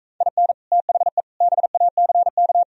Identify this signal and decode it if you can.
Morse code, heard as a keyed tone, about 45 wpm, 700 Hz. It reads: IN THE BACK